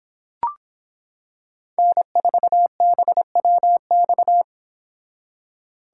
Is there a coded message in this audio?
Yes: Morse code from 1.8 to 4.4 s, heard as a keyed tone, about 26 wpm, 700 Hz. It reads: N4BWX